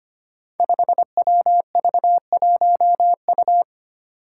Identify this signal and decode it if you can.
Morse code, heard as a keyed tone, about 25 wpm, 700 Hz. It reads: 5WV1U
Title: Five Challenging Letter-Number Combinations 25wpm